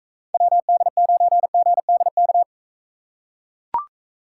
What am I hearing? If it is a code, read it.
Morse code, heard as a keyed tone, about 42 wpm, 700 Hz. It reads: WD9GDK